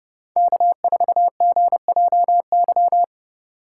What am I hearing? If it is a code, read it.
Morse code, heard as a keyed tone, about 29 wpm, 700 Hz. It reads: K4GJY